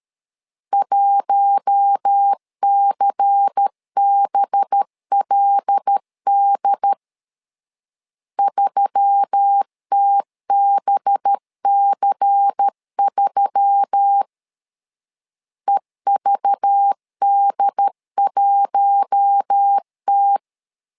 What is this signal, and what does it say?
Morse code, heard as a keyed tone, about 13 wpm, 800 Hz. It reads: 1CBLD 3TBC3 EVD1T
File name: LEZIONI / 5 LBVC / 5 LBVC 12wpm 1955